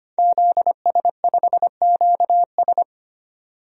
Morse code, heard as a keyed tone, about 25 wpm, 700 Hz. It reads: ZS5QS
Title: DXCC-100 Call Signs - 5 Characters - 25wpm